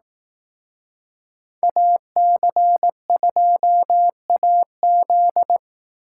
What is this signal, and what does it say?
Morse code, heard as a keyed tone, about 18 wpm, 700 Hz. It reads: AC2AZ
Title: Speed Racing - USA 2x2 Call Signs 15wpm